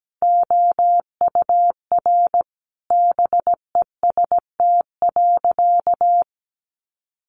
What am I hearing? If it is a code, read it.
Morse code, heard as a keyed tone, about 17 wpm, 700 Hz. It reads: OUR BEST.
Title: Sentences from Top 900 Words 17wpm